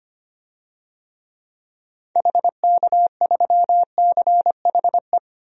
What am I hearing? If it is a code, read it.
Morse code, heard as a keyed tone, about 25 wpm, 700 Hz. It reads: HK3CHE
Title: DXCC-100 Call Signs - 6 Characters - 25wpm